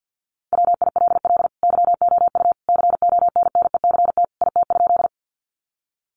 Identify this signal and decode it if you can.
Morse code, heard as a keyed tone, about 50 wpm, 700 Hz. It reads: WILL YOU CONNECT IT?
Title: Sentences from Top 1500 Words 50wpm